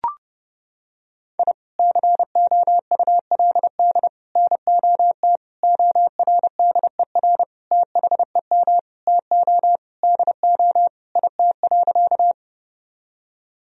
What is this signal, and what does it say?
Morse code, heard as a keyed tone, about 30 wpm, 700 Hz. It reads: I COULD NOT ORDER THEM TO DO IT.
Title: Sentences from Top 1000 Words - Review Part 2 - 30wpm